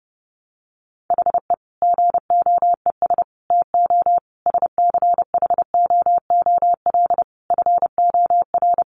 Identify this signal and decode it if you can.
Morse code, heard as a keyed tone, about 30 wpm, 700 Hz. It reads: HE GOES TO SCHOOL FOR
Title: Additional Easy Phrases 30wpm